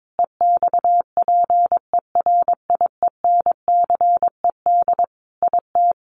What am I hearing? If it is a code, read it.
Morse code, heard as a keyed tone, about 22 wpm, 700 Hz. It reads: EXPERIENCED IT
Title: Sentences from Top 1100 Words 22wpm